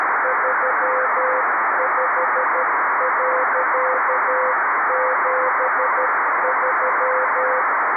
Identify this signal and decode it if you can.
Morse code, heard as a keyed tone, about 13 wpm, 520 Hz, with radio noise. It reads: 35.73
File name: arissat / recordings / ARISSat 11Sept2011 1354utc 60frames